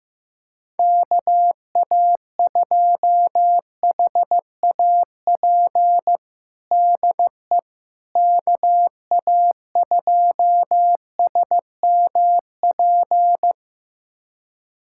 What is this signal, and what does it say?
Morse code, heard as a keyed tone, about 15 wpm, 700 Hz. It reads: KA2HAP DE KA2SMP
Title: USA Callsign DE Callsign - 6 Characters - 15wpm